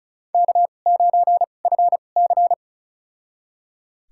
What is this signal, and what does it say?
Morse code, heard as a keyed tone, about 36 wpm, spaced as for 25 wpm, 700 Hz. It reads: K9FC
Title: USA Call Signs - 4 Characters - 2x Character Spacing - 35wpm